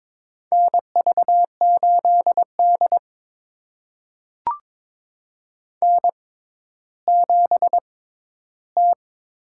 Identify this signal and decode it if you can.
Morse code, heard as a keyed tone, about 22 wpm, spaced as for 12 wpm, 700 Hz. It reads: NV8D N7T
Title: USA Call Signs - 4 Characters - 6x Character Spacing - 22wpm